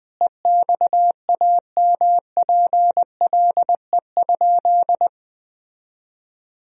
Morse code, heard as a keyed tone, about 21 wpm, 700 Hz. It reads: EXAMPLE?
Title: Sentences from Top 1000 Words - Review Part 1 - 20wpm